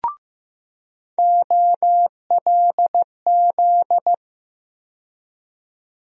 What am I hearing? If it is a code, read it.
Morse code, heard as a keyed tone, about 15 wpm, 700 Hz. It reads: OLZ
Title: USA Call Sign Postfixes - 3 Letters 15wpm